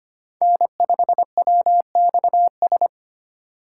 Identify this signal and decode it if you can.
Morse code, heard as a keyed tone, about 25 wpm, 700 Hz. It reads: N5WXS
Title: USA Call Signs - 5 Characters - 4x Character Spacing - 25wpm